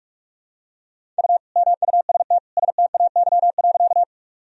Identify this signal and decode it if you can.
Morse code, heard as a keyed tone, about 42 wpm, 700 Hz. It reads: U MUST STAY.
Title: Sentences from Top 500 Words and Abbreviations 45wpm